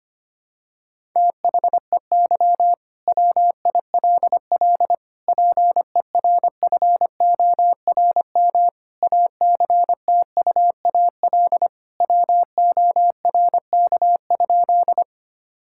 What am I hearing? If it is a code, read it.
Morse code, heard as a keyed tone, about 25 wpm, 700 Hz. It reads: THEY WILL PERFORM ACTUAL WORK?
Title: Sentences from Top 1000 Words - Review Part 5 - 25wpm